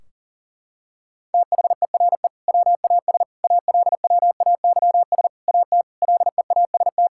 Morse code, heard as a keyed tone, about 40 wpm, 700 Hz. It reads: THERE WAS ALWAYS AT LEAST
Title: Sentences from Top 1600 Words 40wpm